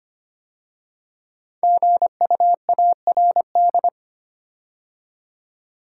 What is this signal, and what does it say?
Morse code, heard as a keyed tone, about 25 wpm, 700 Hz. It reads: GUARD
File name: 5 Letter Words - First 3 Letters Given - 25wpm